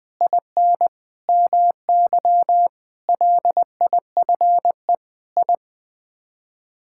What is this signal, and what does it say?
Morse code, heard as a keyed tone, about 20 wpm, 700 Hz. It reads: IN MY LIFE I
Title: Sets of 8 Words - 20wpm